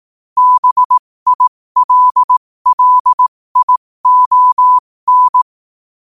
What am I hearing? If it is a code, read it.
Morse code, heard as a keyed tone, about 18 wpm, spaced as for 15 wpm, 1000 Hz. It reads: BILLION
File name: qotdfiles / 15wpm